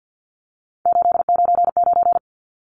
Morse code, heard as a keyed tone, about 50 wpm, 700 Hz. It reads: 899
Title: Numbers - 3-Digit 50wpm